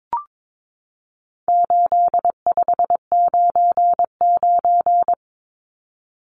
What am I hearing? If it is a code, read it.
Morse code, heard as a keyed tone, about 22 wpm, 700 Hz. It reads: 8599
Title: Numbers - 4-Digit 22wpm